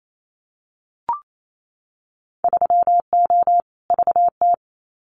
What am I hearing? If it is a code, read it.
Morse code, heard as a keyed tone, about 28 wpm, 700 Hz. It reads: 3O VT
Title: Winter Field Day Exchanges Encore 28wpm